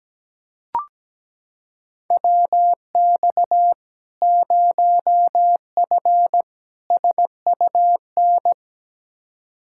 Morse code, heard as a keyed tone, about 17 wpm, 700 Hz. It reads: WX 0F SUN
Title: Practice QSO - Exchange 2 - Info - 17wpm